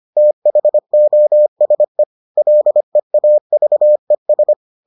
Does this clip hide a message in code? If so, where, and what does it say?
Yes: Morse code from 0.2 to 4.5 s, heard as a keyed tone, about 25 wpm, 600 Hz. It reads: THOSE LEAVES